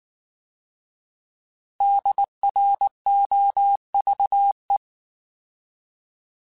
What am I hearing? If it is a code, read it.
Morse code, heard as a keyed tone, about 19 wpm, 800 Hz. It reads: DROVE